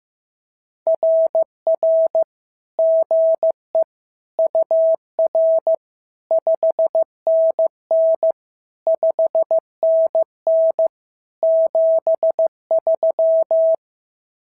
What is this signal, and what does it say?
Morse code, heard as a keyed tone, about 15 wpm, 650 Hz. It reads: RR GE UR 5NN 5NN 73
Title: SOTA QSO Practice 15wpm